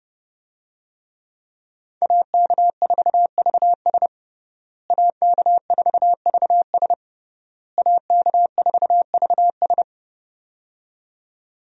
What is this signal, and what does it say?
Morse code, heard as a keyed tone, about 30 wpm, 700 Hz. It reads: AK4VS AK4VS AK4VS
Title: USA Call Signs - 5 Characters - Repeated 3x - 30wpm